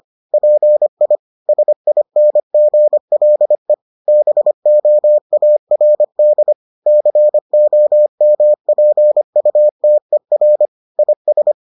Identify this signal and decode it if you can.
Morse code, heard as a keyed tone, about 25 wpm, 600 Hz. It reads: PI SINGLE BOARD COMPUTER IS